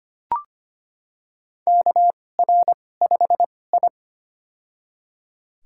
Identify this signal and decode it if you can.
Morse code, heard as a keyed tone, about 25 wpm, spaced as for 18 wpm, 700 Hz. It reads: KR5I
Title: USA Call Signs - 4 Characters - 2x Character Spacing - 25wpm